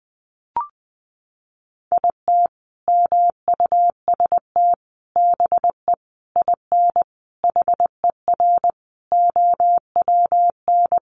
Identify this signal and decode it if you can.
Morse code, heard as a keyed tone, about 20 wpm, 700 Hz. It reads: IT MUST BE IN HER OWN
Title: Sentences from Top 300 Words 20wpm